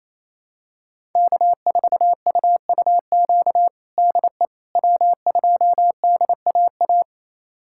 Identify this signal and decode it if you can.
Morse code, heard as a keyed tone, about 28 wpm, 700 Hz. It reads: K4UUQ DE W2DAA
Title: USA Callsign DE Callsign - 5 Characters - 28wpm